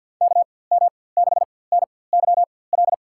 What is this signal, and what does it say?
Morse code, heard as a keyed tone, about 52 wpm, spaced as for 24 wpm, 700 Hz. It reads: KM6NYL
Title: USA Call Signs - 6 Characters - 4x Character Spacing - 50wpm